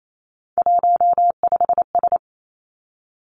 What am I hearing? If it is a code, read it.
Morse code, heard as a keyed tone, about 28 wpm, 690 Hz. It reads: 15S
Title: Three Challenging Letter-Number Combinations 28wpm